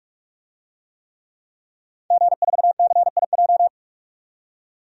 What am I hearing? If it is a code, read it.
Morse code, heard as a keyed tone, about 46 wpm, 700 Hz. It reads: G4CIJ
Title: DXCC-100 Call Signs - 5 Characters - 45wpm